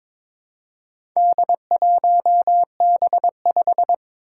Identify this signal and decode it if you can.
Morse code, heard as a keyed tone, about 22 wpm, 700 Hz. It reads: D1B5
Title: Four Challenging Letter-Number Combinations 22wpm